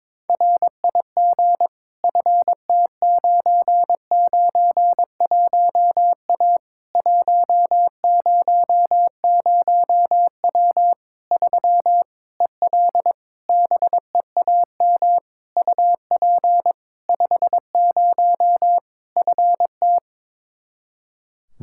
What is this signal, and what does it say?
Morse code, heard as a keyed tone, about 22 wpm, 700 Hz. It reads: RIG FT991A 100W 3 EL BEAM UP 50 FT
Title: K1USN SST QSO Practice 22wpm